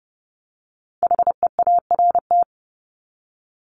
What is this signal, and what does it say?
Morse code, heard as a keyed tone, about 30 wpm, 700 Hz. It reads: HEART
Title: Top 1001-1250 Words 30wpm